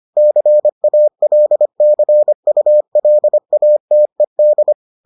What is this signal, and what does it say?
Morse code, heard as a keyed tone, about 25 wpm, 600 Hz. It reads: CALCULATED